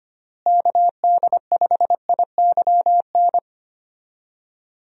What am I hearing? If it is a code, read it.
Morse code, heard as a keyed tone, about 25 wpm, 700 Hz. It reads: KD5IYN
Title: USA Call Signs - 6 Characters - Repeated 3x - 25wpm